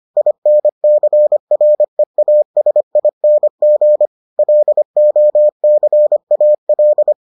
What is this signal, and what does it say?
Morse code, heard as a keyed tone, about 25 wpm, 600 Hz. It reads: INCREASING LOCAL